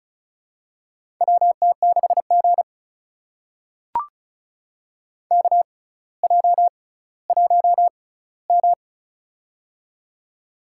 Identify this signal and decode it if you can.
Morse code, heard as a keyed tone, about 36 wpm, spaced as for 18 wpm, 700 Hz. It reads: WT6G KJ1M
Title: USA Call Signs - 4 Characters - 6x Character Spacing - 35wpm